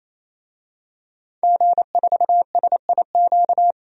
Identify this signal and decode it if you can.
Morse code, heard as a keyed tone, about 27 wpm, 700 Hz. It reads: G4SIQ
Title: DXCC-100 Call Signs - 5 Characters - 28wpm